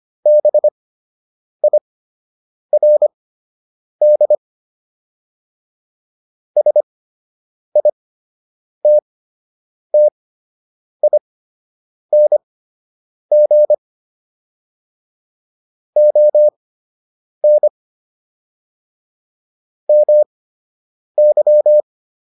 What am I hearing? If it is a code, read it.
Morse code, heard as a keyed tone, about 25 wpm, spaced as for 8 wpm, 600 Hz. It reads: BIRD SITTING ON MY